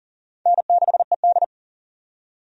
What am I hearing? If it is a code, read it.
Morse code, heard as a keyed tone, about 40 wpm, 700 Hz. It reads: N6ED
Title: USA Call Signs - 4 Characters - 40wpm